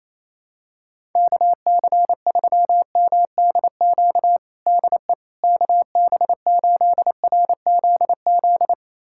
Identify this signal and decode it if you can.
Morse code, heard as a keyed tone, about 28 wpm, 700 Hz. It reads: KC3MDQ DE KB8RZZ
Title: USA Callsign DE Callsign 28wpm